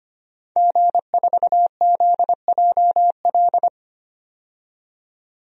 Morse code, heard as a keyed tone, about 25 wpm, 700 Hz. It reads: G4ZJL
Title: DXCC-100 Call Signs - 5 Characters - 25wpm